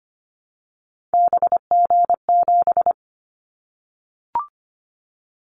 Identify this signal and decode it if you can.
Morse code, heard as a keyed tone, about 25 wpm, 700 Hz. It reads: BG7
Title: Callsign Prefixes for Top 100 DXCC - 25wpm